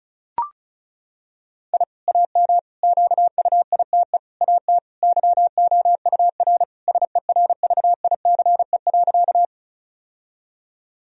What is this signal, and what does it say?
Morse code, heard as a keyed tone, about 34 wpm, 700 Hz. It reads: I AM QUITE AT YOUR SERVICE.
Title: Sentences from Top 1000 Words - Review Part 4 - 35wpm